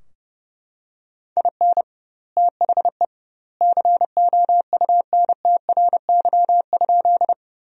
Morse code, heard as a keyed tone, about 30 wpm, 700 Hz. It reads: IN THE COUNTRY?
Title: 2x Word-Spacing - Sentences from Top 300 Words - 30wpm